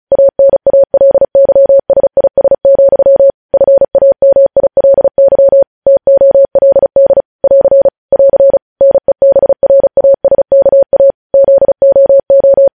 Morse code, heard as a keyed tone, about 36 wpm, 550 Hz. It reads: ANALYSIS, FAMILY TOLD <AR> <AR> NEBRASKA ZOO
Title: News Headlines at 35